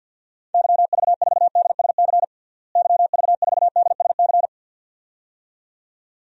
Morse code, heard as a keyed tone, about 50 wpm, 700 Hz. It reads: YV4DSC YV4DSC
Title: DXCC-100 Call Signs - 6 Characters - Repeated 2x - 50wpm